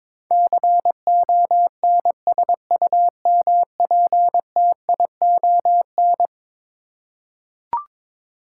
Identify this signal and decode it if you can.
Morse code, heard as a keyed tone, about 22 wpm, 700 Hz. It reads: CONSUMPTION
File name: practice / 22wpm / Eleven Letter Words - 22wpm